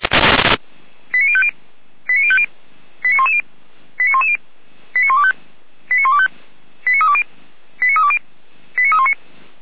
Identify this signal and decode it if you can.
Five-tone selective call (ZVEI), heard as quick runs of five tones, tones of about 70 ms. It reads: 89050 89050 89110 89110 89125 89125 89320 89320 89319